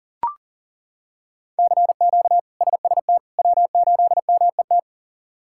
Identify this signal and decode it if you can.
Morse code, heard as a keyed tone, about 40 wpm, 700 Hz. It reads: CQ SST W8MET
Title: K1USN SST QSO Practice - 40wpm